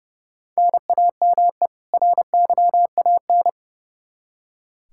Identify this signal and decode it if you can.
Morse code, heard as a keyed tone, about 31 wpm, 700 Hz. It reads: NAME RYAN
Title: Practice QSO - Exchange 1 - Beginning - 30wpm